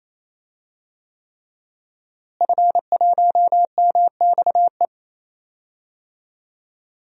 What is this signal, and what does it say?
Morse code, heard as a keyed tone, about 27 wpm, 700 Hz. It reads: F1MXE
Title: DXCC-100 Call Signs - 5 Characters - Repeated 3x - 28wpm